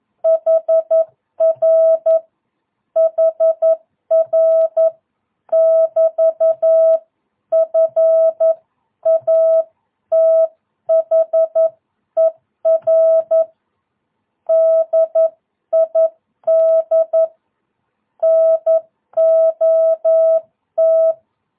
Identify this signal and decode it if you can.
Morse code, heard as a keyed tone, about 11 wpm, 660 Hz. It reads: HR HR <BT> FATHER DID NOT
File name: jh8cbh / 2022cw